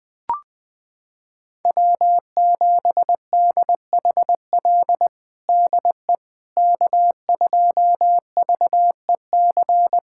Morse code, heard as a keyed tone, about 20 wpm, 700 Hz. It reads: W7DHL DE K2VEC
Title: USA Callsign DE Callsign - 5 Characters - 20wpm